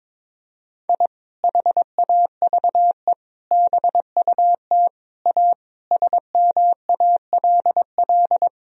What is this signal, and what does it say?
Morse code, heard as a keyed tone, about 22 wpm, 700 Hz. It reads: I HAVE BUT A SMALL
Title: Sentences from Top 1000 Words - Review Part 4 - 22wpm